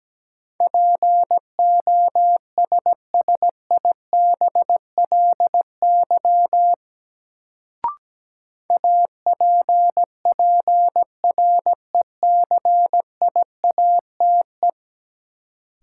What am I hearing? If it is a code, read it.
Morse code, heard as a keyed tone, about 17 wpm, 700 Hz. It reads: POSSIBLY APPRECIATE
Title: Sentences from Top 1300 Words - New Word Drill - 17wpm